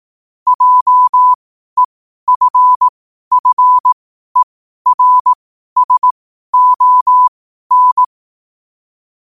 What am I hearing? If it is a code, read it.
Morse code, heard as a keyed tone, about 18 wpm, spaced as for 12 wpm, 1000 Hz. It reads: JEFFERSON
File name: qotdfiles / 12wpm